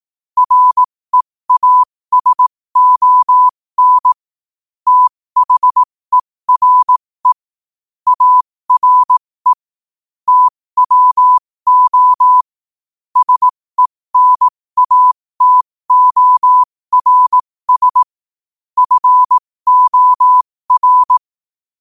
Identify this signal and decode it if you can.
Morse code, heard as a keyed tone, about 18 wpm, spaced as for 15 wpm, 1000 Hz. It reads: REASON THERE ARE TWO SENATORS FOR
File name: qotdfiles / 15wpm